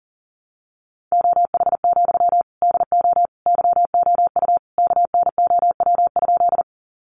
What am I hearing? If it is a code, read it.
Morse code, heard as a keyed tone, about 40 wpm, 700 Hz. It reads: OH, DO YOU KNOW?
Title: Sentences from Top 500 Words 40wpm